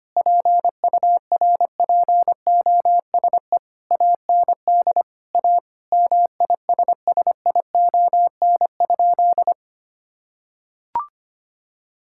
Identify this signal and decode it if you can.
Morse code, heard as a keyed tone, about 25 wpm, 700 Hz. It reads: PURPOSE AND A MISSION?
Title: Sentences from Top 1600 Words 25wpm